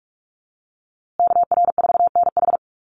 Morse code, heard as a keyed tone, about 44 wpm, 700 Hz. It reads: KR4NH
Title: USA Call Signs Repeated 3x 45wpm